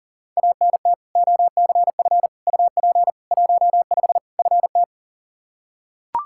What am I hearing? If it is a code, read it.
Morse code, heard as a keyed tone, about 40 wpm, 700 Hz. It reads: ANT OCF UP 15 FT
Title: Practice QSO - Exchange 2 - Info - 40wpm